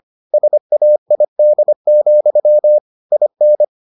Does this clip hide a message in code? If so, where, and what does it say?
Yes: Morse code from 0.3 to 3.6 s, heard as a keyed tone, about 25 wpm, 600 Hz. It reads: SAID, IN